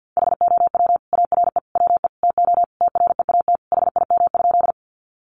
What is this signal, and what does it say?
Morse code, heard as a keyed tone, about 50 wpm, 700 Hz. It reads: HOW ARE WE TO TREAT HIM?
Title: Sentences from Top 1300 Words 50wpm